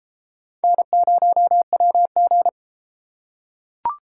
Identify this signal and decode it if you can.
Morse code, heard as a keyed tone, about 33 wpm, 700 Hz. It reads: N0WG